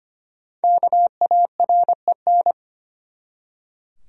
Morse code, heard as a keyed tone, about 25 wpm, 700 Hz. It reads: KAREN
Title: USA Common Names - Max 5 Letters - 25wpm